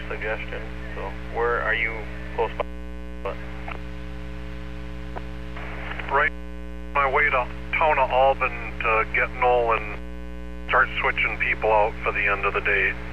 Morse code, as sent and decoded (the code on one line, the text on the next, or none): none